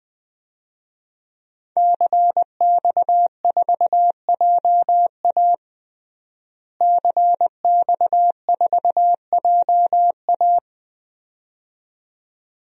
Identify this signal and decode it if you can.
Morse code, heard as a keyed tone, about 20 wpm, 700 Hz. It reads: CX4JA CX4JA
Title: DXCC-100 Call Signs - 5 Characters - Repeated 2x - 20wpm